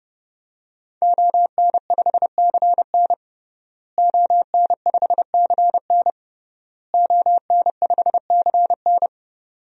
Morse code, heard as a keyed tone, about 32 wpm, 700 Hz. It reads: ON5CN ON5CN ON5CN
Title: DXCC-100 Call Signs - 5 Characters - Repeated 3x - 30wpm